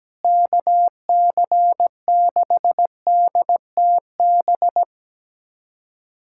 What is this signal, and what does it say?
Morse code, heard as a keyed tone, about 17 wpm, 700 Hz. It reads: KC6DTB
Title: USA Callsign DE Callsign - 6 Characters - 17wpm